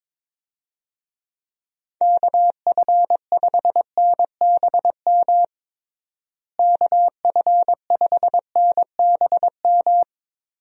Morse code, heard as a keyed tone, about 22 wpm, 700 Hz. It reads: KF5NBM KF5NBM
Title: USA Call Signs - 6 Characters - Repeated 2x - 22wpm